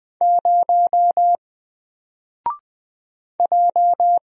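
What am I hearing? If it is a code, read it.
Morse code, heard as a keyed tone, about 20 wpm, 700 Hz. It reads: 0 J
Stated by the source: Single letter and number Morse drill at 20 wpm, from N4LCD.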